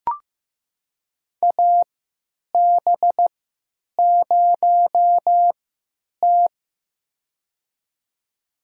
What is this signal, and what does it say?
Morse code, heard as a keyed tone, about 15 wpm, spaced as for 9 wpm, 700 Hz. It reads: AB0T
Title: USA Call Signs - 4 Characters - 3x Character Spacing - 15wpm